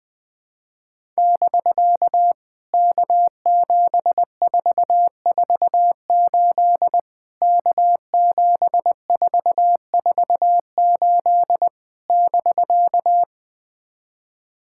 Morse code, heard as a keyed tone, about 21 wpm, 700 Hz. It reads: <BK> K7448 K7448 <BK>
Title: POTA QSO Practice - 20wpm